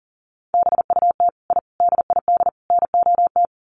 Morse code, heard as a keyed tone, about 40 wpm, 700 Hz. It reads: BUT I DID NOT
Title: Sets of 4 Words 40wpm